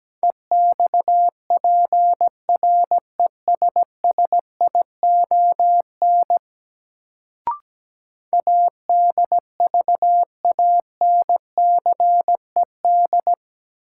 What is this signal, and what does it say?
Morse code, heard as a keyed tone, about 17 wpm, 700 Hz. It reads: EXPRESSION ADVANCED
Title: Sentences from Top 1200 Words - New Word Drill Alt - 17wpm